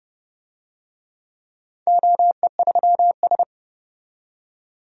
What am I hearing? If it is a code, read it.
Morse code, heard as a keyed tone, about 30 wpm, 700 Hz. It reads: OE3S
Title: DXCC-100 Call Signs - 4 Characters - 30wpm